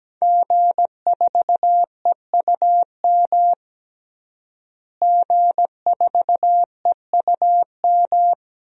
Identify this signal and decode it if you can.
Morse code, heard as a keyed tone, about 17 wpm, 700 Hz. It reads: G4EUM G4EUM
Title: DXCC-100 Call Signs - 5 Characters - Repeated 3x - 17wpm